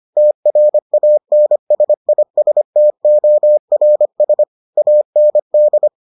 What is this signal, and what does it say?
Morse code, heard as a keyed tone, about 25 wpm, 600 Hz. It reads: TRANSISTORS AND